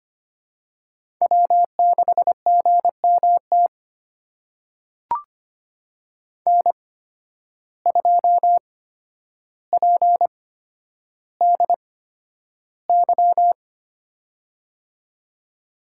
Morse code, heard as a keyed tone, about 25 wpm, spaced as for 11 wpm, 700 Hz. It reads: W6GMT N2PDY
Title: USA Call Signs - 5 Characters - 8x Character Spacing - 25wpm